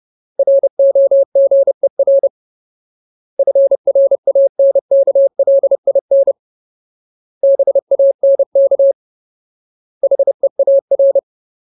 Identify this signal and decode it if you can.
Morse code, heard as a keyed tone, about 30 wpm, 550 Hz. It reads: ROGER FRANKLIN BANK HEAR
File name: Ultimate Word List / 30WPM Ultimate